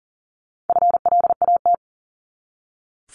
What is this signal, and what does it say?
Morse code, heard as a keyed tone, about 40 wpm, 700 Hz. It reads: FLAT
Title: Top 751-1000 Words 40wpm